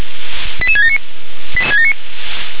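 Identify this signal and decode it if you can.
Five-tone selective call (ZVEI), heard as quick runs of five tones, tones of about 70 ms. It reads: 88679
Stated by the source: ZVEI five-tone call 88774 received by SDR.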